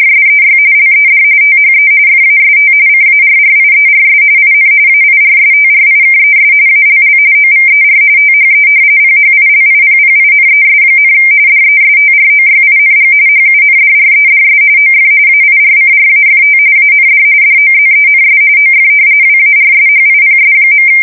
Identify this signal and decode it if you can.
RTTY, about 45.45 baud. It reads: SEY LED THE FAST CARRIERS IN AIRSTRIKES ALL OVER THE PHILIPPINES, DESTROYING HUNDREDS OF AIRCRAFT AND DOZENS OF WARSHIPS.
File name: fsk170 / battleshipnj